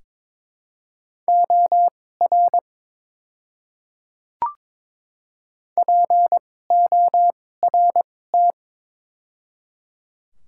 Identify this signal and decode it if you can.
Morse code, heard as a keyed tone, about 22 wpm, spaced as for 16 wpm, 700 Hz. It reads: OR PORT
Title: 2x Character Spacing - Top 300 Words - 22wpm